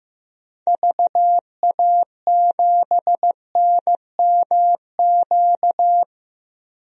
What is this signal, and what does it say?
Morse code, heard as a keyed tone, about 15 wpm, 690 Hz. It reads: VA7NMQ
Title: DXCC-100 Call Signs - 6 Characters - 15wpm